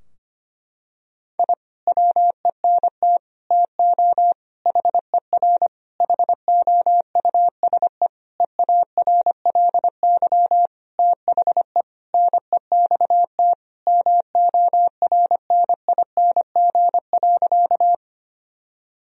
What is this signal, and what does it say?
Morse code, heard as a keyed tone, about 25 wpm, 700 Hz. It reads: I WENT TO HER HOUSE EARLY THE NEXT MORNING.
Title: Sentences from Top 1000 Words - Review Part 2 - 25wpm